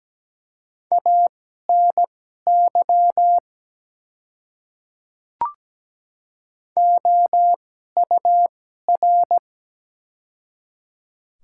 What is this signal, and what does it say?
Morse code, heard as a keyed tone, about 17 wpm, spaced as for 12 wpm, 700 Hz. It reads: ANY OUR
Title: 2x Character Spacing - Top 200 Words - 17wpm